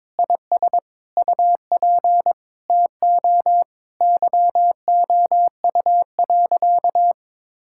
Morse code, heard as a keyed tone, about 22 wpm, 700 Hz. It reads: IS UP TO YOU.
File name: Sentences with 7-Word Phrases - 22wpm